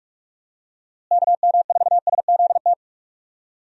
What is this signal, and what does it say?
Morse code, heard as a keyed tone, about 46 wpm, 700 Hz. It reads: KM4SZT
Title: USA Call Signs - 6 Characters - Repeated 3x - 45wpm